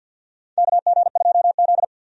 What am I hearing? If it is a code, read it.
Morse code, heard as a keyed tone, about 52 wpm, 700 Hz. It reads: KG2Z
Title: USA Call Signs - 4 Characters - 50wpm